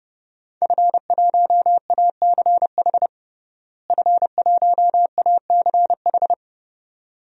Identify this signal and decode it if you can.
Morse code, heard as a keyed tone, about 30 wpm, 700 Hz. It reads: F1ACH F1ACH
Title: DXCC-100 Call Signs - 5 Characters - Repeated 2x - 30wpm